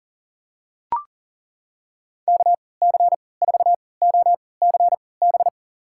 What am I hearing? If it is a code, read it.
Morse code, heard as a keyed tone, about 40 wpm, spaced as for 23 wpm, 700 Hz. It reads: KC4OCB
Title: USA Call Signs - 6 Characters - 3x Character Spacing - 40wpm